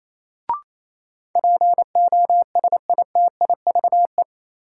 Morse code, heard as a keyed tone, about 28 wpm, 700 Hz. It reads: POSITIVE
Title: Sentences from Top 1000 Words - Review Part 5 - New Word Drill - 28wpm